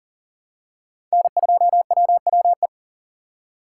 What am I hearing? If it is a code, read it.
Morse code, heard as a keyed tone, about 40 wpm, 700 Hz. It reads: N2WWE
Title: USA Call Signs - 2x Character Spacing - 40wpm